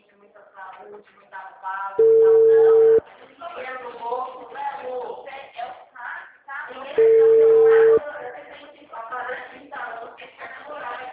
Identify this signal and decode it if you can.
Ringback tone (China, 450 Hz), with speech.